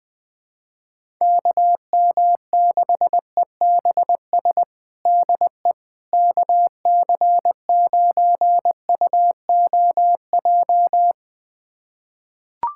Morse code, heard as a keyed tone, about 20 wpm, 700 Hz. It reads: KM6EBS DE KC9UOJ
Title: USA Callsign DE Callsign - 6 Characters - 20wpm